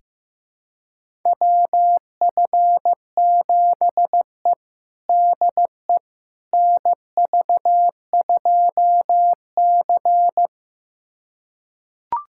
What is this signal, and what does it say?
Morse code, heard as a keyed tone, about 15 wpm, 700 Hz. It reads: WF7E DE NV2C
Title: USA Callsign DE Callsign - 4 Characters - 15wpm